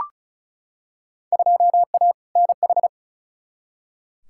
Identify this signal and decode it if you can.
Morse code, heard as a keyed tone, about 36 wpm, 700 Hz. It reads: 2A NH
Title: Field Day Exchanges Encore - 35wpm